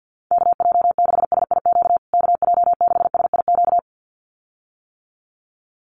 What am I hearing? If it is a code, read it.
Morse code, heard as a keyed tone, about 50 wpm, 700 Hz. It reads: KJ6SIQ KJ6SIQ
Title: USA Call Signs Repeated 2x 50wpm